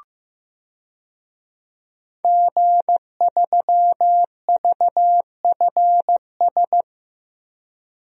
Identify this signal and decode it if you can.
Morse code, heard as a keyed tone, about 15 wpm, 700 Hz. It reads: G3VFS
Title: DXCC-100 Call Signs - 5 Characters - 15wpm